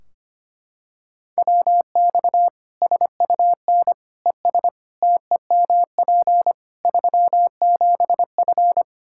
Morse code, heard as a keyed tone, about 25 wpm, 700 Hz. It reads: WX SUN ES TEMP 37F
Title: Practice QSO - Exchange 2 - Info - 25wpm